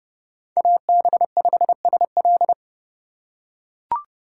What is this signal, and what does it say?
Morse code, heard as a keyed tone, about 30 wpm, 700 Hz. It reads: AB5SL
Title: USA Call Signs - 5 Characters - 3x Character Spacing - 30wpm